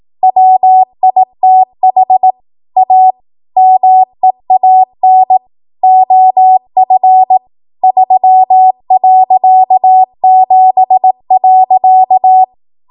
Morse code, heard as a keyed tone, about 18 wpm, 750 Hz. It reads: WITH A MEAN OF 3.7.
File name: Morse / 18WPM